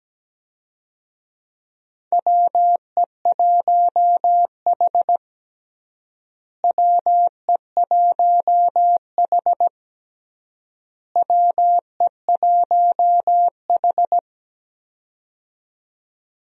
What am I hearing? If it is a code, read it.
Morse code, heard as a keyed tone, about 17 wpm, 700 Hz. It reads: WE1H WE1H WE1H
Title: USA Call Signs - 4 Characters - Repeated 3x - 17wpm